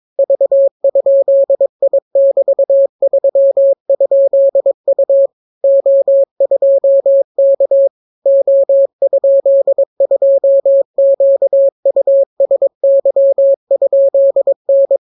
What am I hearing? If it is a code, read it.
Morse code, heard as a keyed tone, about 22 wpm, 550 Hz. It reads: V?I<BT>3?U O2K O?2QUSY?N